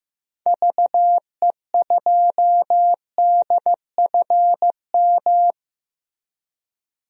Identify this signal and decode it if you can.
Morse code, heard as a keyed tone, about 15 wpm, 700 Hz. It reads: VE2DFM
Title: DXCC-100 Call Signs - 6 Characters - 15wpm